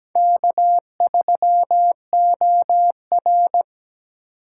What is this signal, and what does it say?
Morse code, heard as a keyed tone, about 20 wpm, 700 Hz. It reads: K3OR
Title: Speed Racing - USA 1x2 Call Signs 17wpm